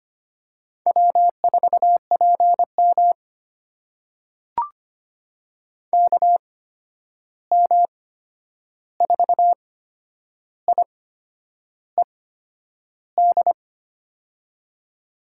Morse code, heard as a keyed tone, about 25 wpm, spaced as for 9 wpm, 700 Hz. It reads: W4PM KM4IED
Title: USA Call Signs - 8x Character Spacing - 25wpm